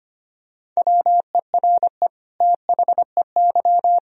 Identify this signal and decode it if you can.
Morse code, heard as a keyed tone, about 25 wpm, 700 Hz. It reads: WERE THEY
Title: Sentences from Top 1000 Words - Review Part 3 - 25wpm